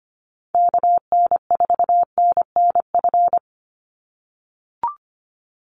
Morse code, heard as a keyed tone, about 25 wpm, 700 Hz. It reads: KN4NNF
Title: USA Call Signs 25wpm